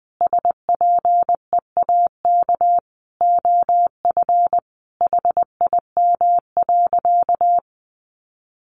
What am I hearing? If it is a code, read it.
Morse code, heard as a keyed tone, about 20 wpm, 700 Hz. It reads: SPEAK OF HIM.
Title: Sentences from Top 1100 Words 20wpm